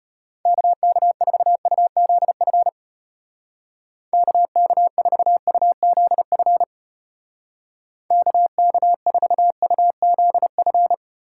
Morse code, heard as a keyed tone, about 32 wpm, 700 Hz. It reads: KK4UZF KK4UZF KK4UZF